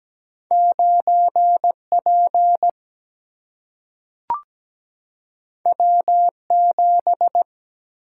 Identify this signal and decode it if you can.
Morse code, heard as a keyed tone, about 17 wpm, 700 Hz. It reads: 9P W7
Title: Single-Letter Single-Number 17wpm